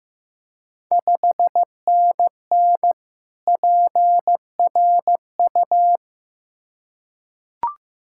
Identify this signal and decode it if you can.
Morse code, heard as a keyed tone, about 15 wpm, 700 Hz. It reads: 5NN PRU
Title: Ontario QSO Party Exchange 15wpm